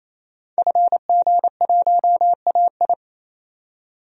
Morse code, heard as a keyed tone, about 28 wpm, 700 Hz. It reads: FG1AI
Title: DXCC-100 Call Signs - 5 Characters - 28wpm